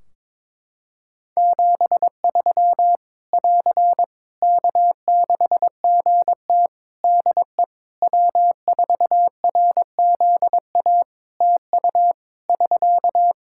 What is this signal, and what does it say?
Morse code, heard as a keyed tone, about 22 wpm, 700 Hz. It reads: 73 <AR> K6GT DE W4RZA TU <SK>
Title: Practice QSO - Exchange 4 - Ending - 22wpm